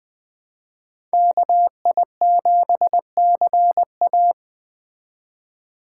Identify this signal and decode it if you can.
Morse code, heard as a keyed tone, about 19 wpm, 700 Hz. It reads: KI7CA